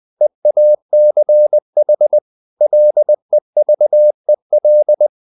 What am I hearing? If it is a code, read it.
Morse code, heard as a keyed tone, about 20 wpm, 600 Hz. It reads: EACH LEVEL
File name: the old man and the sea / the old man and the sea part 1 20wpm